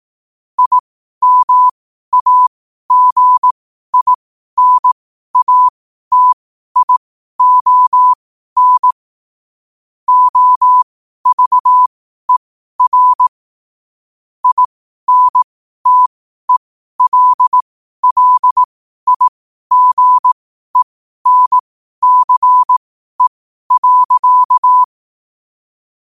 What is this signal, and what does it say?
Morse code, heard as a keyed tone, about 18 wpm, spaced as for 12 wpm, 1000 Hz. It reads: IMAGINATION OVER INTELLIGENCE.
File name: qotdfiles / 12wpm